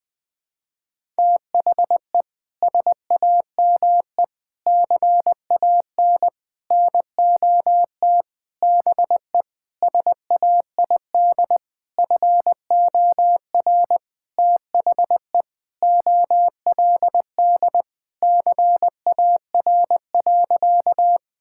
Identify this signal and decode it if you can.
Morse code, heard as a keyed tone, about 21 wpm, 700 Hz. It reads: THE SAME CAN NOT BE SAID FOR THE OLD CAR.
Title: Sentences with 8-Word Phrases - Spoken Beginning - 20wpm